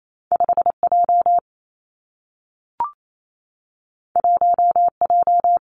Morse code, heard as a keyed tone, about 28 wpm, 700 Hz. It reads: 5 J 1 J
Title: Two Challenging Letter-Number Combinations 28wpm